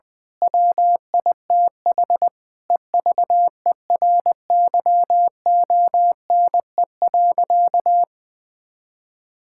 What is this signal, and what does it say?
Morse code, heard as a keyed tone, about 19 wpm, 700 Hz. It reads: WITH EVERYONE.
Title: Sentences from Top 1000 Words - Review Part 5 - 20wpm